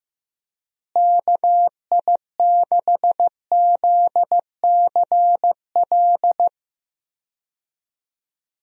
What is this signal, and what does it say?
Morse code, heard as a keyed tone, about 15 wpm, 700 Hz. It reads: KI6ZCL
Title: USA Call Signs - 6 Characters - Repeated 2x - 15wpm